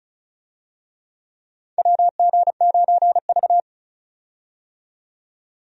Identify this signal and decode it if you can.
Morse code, heard as a keyed tone, about 36 wpm, 700 Hz. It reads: WG9V
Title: USA Call Signs - 4 Characters - 35wpm